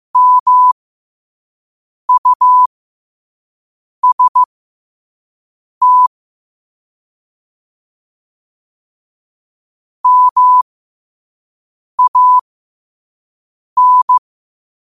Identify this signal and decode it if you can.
Morse code, heard as a keyed tone, about 15 wpm, spaced as for 5 wpm, 1000 Hz. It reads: MUST MAN
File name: qotdfiles / 5wpm